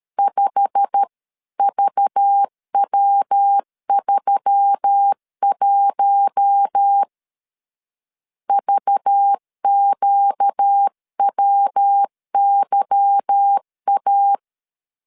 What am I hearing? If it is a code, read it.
Morse code, heard as a keyed tone, about 13 wpm, 800 Hz. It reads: 5VW31 VQWYA
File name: CW / DET / 1 9 1500C S C 12wpm 3811